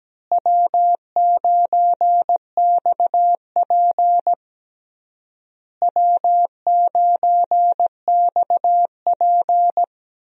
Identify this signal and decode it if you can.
Morse code, heard as a keyed tone, about 17 wpm, 700 Hz. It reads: W9XP W9XP
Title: USA Call Signs - 4 Characters - Repeated 3x - 17wpm